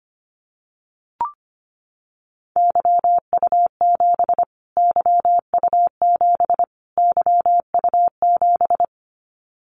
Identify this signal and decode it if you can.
Morse code, heard as a keyed tone, about 25 wpm, 700 Hz. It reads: YU7 YU7 YU7
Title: Callsign Prefixes for Top 25 DXCC - Repeated 3x - 25wpm